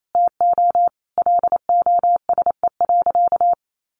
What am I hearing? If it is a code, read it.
Morse code, heard as a keyed tone, about 28 wpm, 700 Hz. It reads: TO LOSE.
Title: Sentences from Top 1100 Words 28wpm